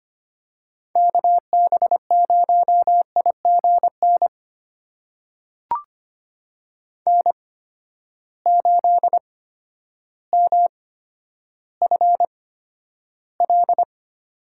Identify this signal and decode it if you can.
Morse code, heard as a keyed tone, about 25 wpm, spaced as for 11 wpm, 700 Hz. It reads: KB0IGN N8MFL
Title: USA Call Signs - 8x Character Spacing - 25wpm